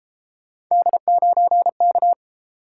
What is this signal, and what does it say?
Morse code, heard as a keyed tone, about 29 wpm, 700 Hz. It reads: D9K